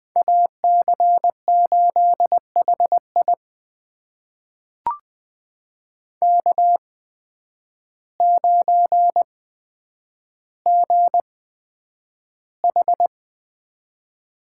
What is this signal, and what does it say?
Morse code, heard as a keyed tone, about 20 wpm, spaced as for 9 wpm, 700 Hz. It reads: AC8HI K9GH
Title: USA Call Signs - 5 Characters - 8x Character Spacing - 20wpm